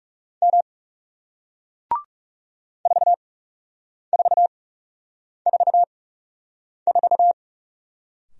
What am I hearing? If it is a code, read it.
Morse code, heard as a keyed tone, about 40 wpm, spaced as for 8 wpm, 700 Hz. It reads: M 4444